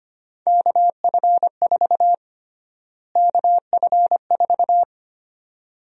Morse code, heard as a keyed tone, about 25 wpm, 700 Hz. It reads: KF4 KF4
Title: USA Call Sign Prefixes - Repeated 2x - 25wpm